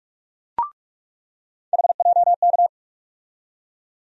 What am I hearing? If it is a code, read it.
Morse code, heard as a keyed tone, about 46 wpm, 700 Hz. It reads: HJK
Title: USA Call Sign Postfixes - 3 Letters 45wpm